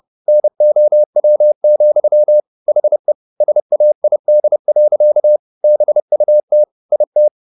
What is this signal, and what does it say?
Morse code, heard as a keyed tone, about 30 wpm, 600 Hz. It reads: NOW, HE SAID. BUT IT